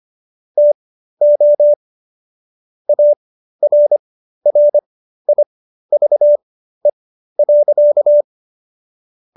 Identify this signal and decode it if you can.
Morse code, heard as a keyed tone, about 25 wpm, spaced as for 13 wpm, 590 Hz. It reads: TO ARRIVE.